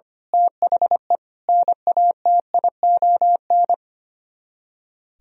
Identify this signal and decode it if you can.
Morse code, heard as a keyed tone, about 24 wpm, 700 Hz. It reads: THE NATION
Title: Speed-Racing - Sets of 5 Words 25wpm